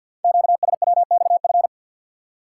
Morse code, heard as a keyed tone, about 46 wpm, 700 Hz. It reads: QSWXF
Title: Code Groups - 5 Letters - 50wpm